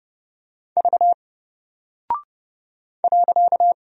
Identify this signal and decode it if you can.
Morse code, heard as a keyed tone, about 31 wpm, 700 Hz. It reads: V .
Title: Single Character (Letter:Number:Punctuation:Prosign) Alt 30wpm